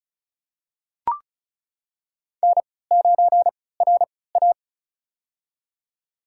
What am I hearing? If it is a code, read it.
Morse code, heard as a keyed tone, about 36 wpm, spaced as for 20 wpm, 700 Hz. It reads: N9RA
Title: USA Call Signs - 4 Characters - 3x Character Spacing - 35wpm